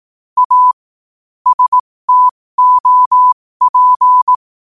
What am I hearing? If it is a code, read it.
Morse code, heard as a keyed tone, about 18 wpm, spaced as for 15 wpm, 1000 Hz. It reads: A STOP
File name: qotdfiles / 15wpm